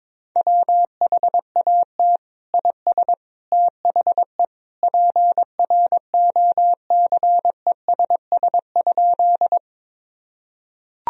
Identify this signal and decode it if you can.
Morse code, heard as a keyed tone, about 22 wpm, 700 Hz. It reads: WHAT IS THE PROCESS?
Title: Sentences from Top 1000 Words - Review Part 3 - 22wpm